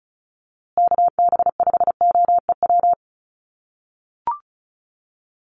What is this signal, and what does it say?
Morse code, heard as a keyed tone, about 34 wpm, 700 Hz. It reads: KB5OEW
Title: USA Call Signs Repeated 3x 35wpm